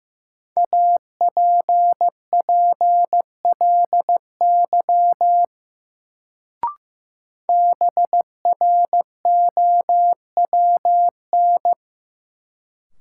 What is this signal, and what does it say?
Morse code, heard as a keyed tone, about 15 wpm, 700 Hz. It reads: APPLY BROWN
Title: Sentences from Top 1200 Words - New Word Drill Alt - 15wpm